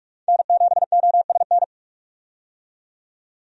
Morse code, heard as a keyed tone, about 47 wpm, 700 Hz. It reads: N7OSN
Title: USA Call Signs - 5 Characters - 45wpm